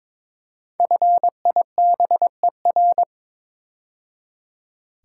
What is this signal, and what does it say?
Morse code, heard as a keyed tone, about 23 wpm, 700 Hz. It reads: FIBER
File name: 5 Letter Words - First 3 Letters Given - 22wpm